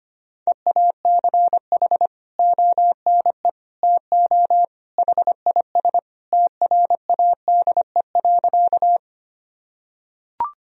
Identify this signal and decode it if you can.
Morse code, heard as a keyed tone, about 25 wpm, 700 Hz. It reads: EACH ONE TO HIS TRADE.
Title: Sentences from Top 1000 Words - Review Part 4 - 25wpm